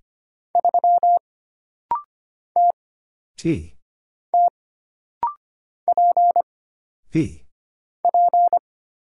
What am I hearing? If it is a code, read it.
Morse code, heard as a keyed tone, about 25 wpm, 700 Hz. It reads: 3 T T P P
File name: Single Character (Letter:Number:Punctuation:Prosign) - Rapid-Fire - 25wpm